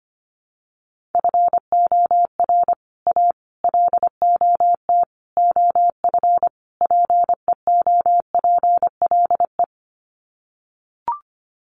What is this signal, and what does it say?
Morse code, heard as a keyed tone, about 25 wpm, 700 Hz. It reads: FOR A LOT OF PEOPLE
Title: Sets of 5 Words Encore 25wpm